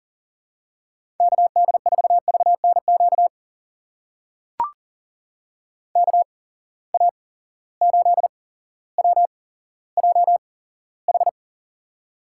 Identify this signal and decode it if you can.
Morse code, heard as a keyed tone, about 39 wpm, spaced as for 17 wpm, 700 Hz. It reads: KD4VNQ KA8WJH
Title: USA Call Signs - 6 Characters - 8x Character Spacing - 40wpm